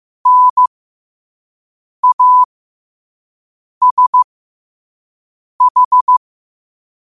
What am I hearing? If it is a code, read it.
Morse code, heard as a keyed tone, about 15 wpm, spaced as for 5 wpm, 1000 Hz. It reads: NASH
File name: qotdfiles / 5wpm